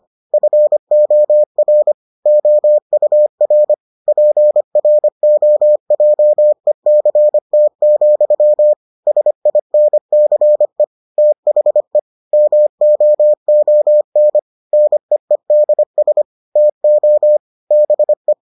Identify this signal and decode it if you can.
Morse code, heard as a keyed tone, about 25 wpm, 600 Hz. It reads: FOR OUR PROJECT, SINCE THE MOON NEEDS TO BE